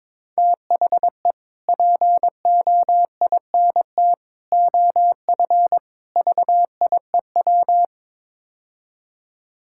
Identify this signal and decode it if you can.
Morse code, heard as a keyed tone, about 22 wpm, 700 Hz. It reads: THE POINT OF VIEW
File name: practice / 22wpm / Build A Phrase - 5-Word Phrases - 22wpm